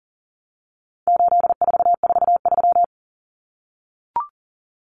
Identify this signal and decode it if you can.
Morse code, heard as a keyed tone, about 40 wpm, 700 Hz. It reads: 8443
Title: Common Cut Numbers - 4-Digit 40wpm